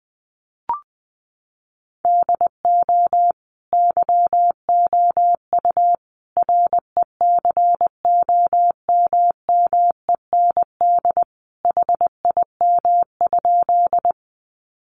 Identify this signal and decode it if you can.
Morse code, heard as a keyed tone, about 20 wpm, 700 Hz. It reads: DO YOU RECOMMEND HIM?
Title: Sentences from Top 1200 Words 20wpm